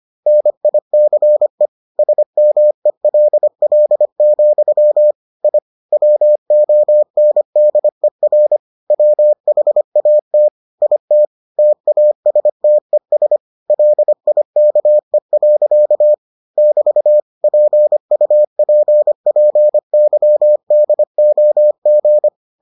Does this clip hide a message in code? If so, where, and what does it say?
Yes: Morse code from 0.3 to 22.3 s, heard as a keyed tone, about 25 wpm, 600 Hz. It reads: NICE SMELL, I WONDER WHAT IT TASTES LIKE. <BT> PUPPYDOG